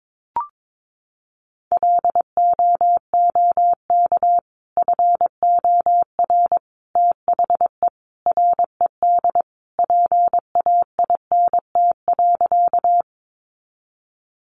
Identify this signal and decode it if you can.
Morse code, heard as a keyed tone, about 22 wpm, 700 Hz. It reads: LOOK FOR THE RED PAINT.
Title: A Sound Of Thunder 22wpm